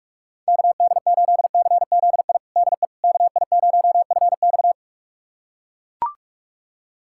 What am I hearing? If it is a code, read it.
Morse code, heard as a keyed tone, about 41 wpm, 700 Hz. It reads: KD8CZI DE KI0FX
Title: USA Callsign DE Callsign 45wpm